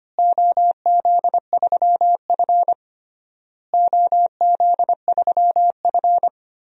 Morse code, heard as a keyed tone, about 25 wpm, 700 Hz. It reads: OZ3F OZ3F
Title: DXCC-100 Call Signs - 4 Characters - Repeated 3x - 25wpm